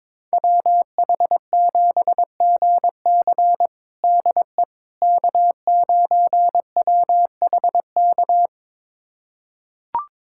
Morse code, heard as a keyed tone, about 22 wpm, 700 Hz. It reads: WH7GC DE K9WHK
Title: USA Callsign DE Callsign - 5 Characters - 22wpm